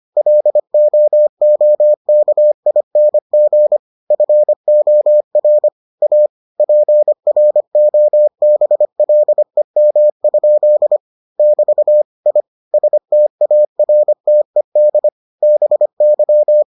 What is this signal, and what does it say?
Morse code, heard as a keyed tone, about 25 wpm, 600 Hz. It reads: LOOKING FOR A PROBLEM? <BT> I STARTED BY